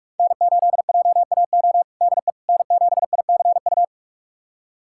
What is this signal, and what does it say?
Morse code, heard as a keyed tone, about 46 wpm, 700 Hz. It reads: N8JAO DE N7ICU
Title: USA Callsign DE Callsign 45wpm